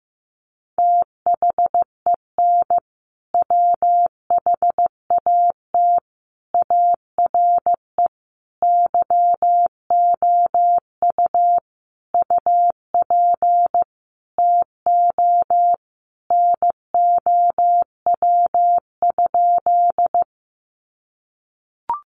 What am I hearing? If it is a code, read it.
Morse code, heard as a keyed tone, about 15 wpm, 700 Hz. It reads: THEN WHAT ARE YOU UP TO NOW?
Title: Sentences from Top 100 Words 15wpm